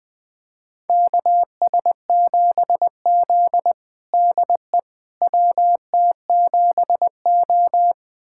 Morse code, heard as a keyed tone, about 20 wpm, 700 Hz. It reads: KS7Z DE WT7O
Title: USA Callsign DE Callsign - 4 Characters - 20wpm